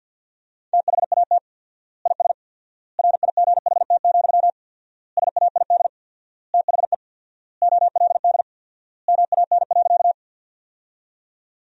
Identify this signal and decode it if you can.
Morse code, heard as a keyed tone, about 50 wpm, 700 Hz. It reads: THAT IS RIGHT, SAID THE OLD MAN.
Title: 4x Word-Spacing - Sentences from Top 200 Words - 50wpm